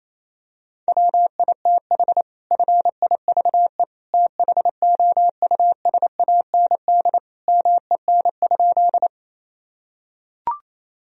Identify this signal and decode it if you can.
Morse code, heard as a keyed tone, about 28 wpm, 700 Hz. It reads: WITH FIVE THOUSAND MEN?
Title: Sentences from Top 1000 Words - Review Part 3 - 28wpm